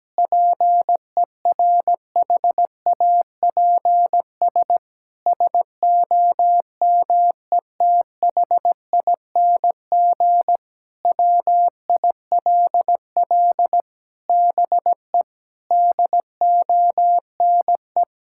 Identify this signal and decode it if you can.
Morse code, heard as a keyed tone, about 17 wpm, 700 Hz. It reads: PERHAPS SOMETHING WILL BE DONE
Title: Sentences from Top 1600 Words 17wpm